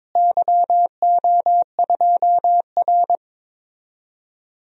Morse code, heard as a keyed tone, about 22 wpm, 700 Hz. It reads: YO2R
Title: DXCC-100 Call Signs - 4 Characters - Repeated 3x - 22wpm